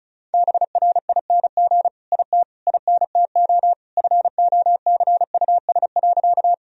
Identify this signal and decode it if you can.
Morse code, heard as a keyed tone, about 36 wpm, 700 Hz. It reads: BRING IT INTO FOCUS.
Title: Sentences from Top 1000 Words - Review Part 4 - 35wpm